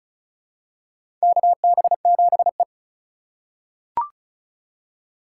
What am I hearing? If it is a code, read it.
Morse code, heard as a keyed tone, about 36 wpm, 700 Hz. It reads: KB7E
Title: USA Call Signs - 4 Characters - 4x Character Spacing - 35wpm